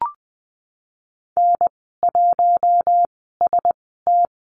Morse code, heard as a keyed tone, about 20 wpm, 700 Hz. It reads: N 1 S T
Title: 076 Lesson 13g - Letter L - Call Signs 20wpm